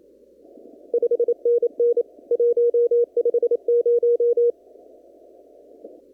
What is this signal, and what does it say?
Morse code, heard as a keyed tone, about 27 wpm, 480 Hz. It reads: 5NN 150
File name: CW / kt5j wpx cw-op-k5pi